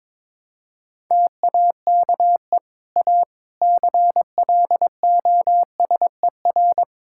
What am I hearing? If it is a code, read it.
Morse code, heard as a keyed tone, about 22 wpm, 700 Hz. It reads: TAKE A CLOSER
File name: practice / 22wpm / Sets of 6 Words - 22wpm